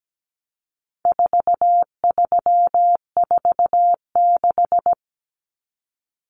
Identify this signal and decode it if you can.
Morse code, heard as a keyed tone, about 17 wpm, 700 Hz. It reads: 4346
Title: Numbers - 4-Digit 17wpm